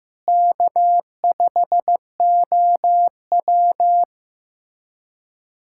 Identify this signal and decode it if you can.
Morse code, heard as a keyed tone, about 15 wpm, 700 Hz. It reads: K5OW
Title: USA Callsign DE Callsign - 4 Characters - 15wpm